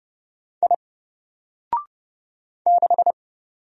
Morse code, heard as a keyed tone, about 30 wpm, 700 Hz. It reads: I 6
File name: Single-Letter-Number-30wpm